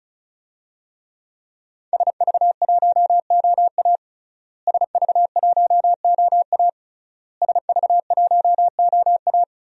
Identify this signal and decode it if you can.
Morse code, heard as a keyed tone, about 36 wpm, 700 Hz. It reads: SV1OA SV1OA SV1OA
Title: DXCC-100 Call Signs - 5 Characters - Repeated 3x - 35wpm